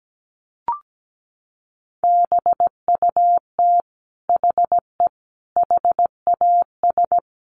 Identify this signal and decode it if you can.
Morse code, heard as a keyed tone, about 17 wpm, 700 Hz. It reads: BUT HE HAS
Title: Sentences from Top 1300 Words 17wpm